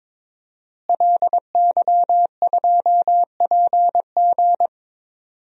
Morse code, heard as a keyed tone, about 22 wpm, 700 Hz. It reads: LY2PG
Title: DXCC-100 Call Signs - 5 Characters - Repeated 2x - 22wpm